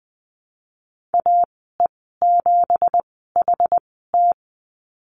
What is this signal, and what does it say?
Morse code, heard as a keyed tone, about 20 wpm, spaced as for 14 wpm, 700 Hz. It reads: AE7HT
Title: 245 Lesson 39g - Letter z - Call Signs 20wpm